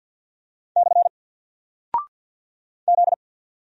Morse code, heard as a keyed tone, about 48 wpm, 700 Hz. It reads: / Z
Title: Single Character (Letter:Number:Punctuation:Prosign) 50wpm